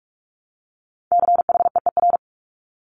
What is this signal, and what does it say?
Morse code, heard as a keyed tone, about 46 wpm, 700 Hz. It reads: CHEER
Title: Multi-letter Phonemes Repeated 3x 45wpm